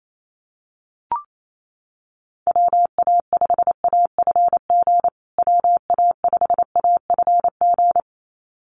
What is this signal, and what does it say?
Morse code, heard as a keyed tone, about 28 wpm, 700 Hz. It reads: WA5AFG WA5AFG
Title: USA Call Signs Repeated 2x 28wpm